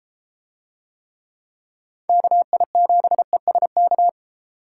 Morse code, heard as a keyed tone, about 33 wpm, 700 Hz. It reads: KI7ESK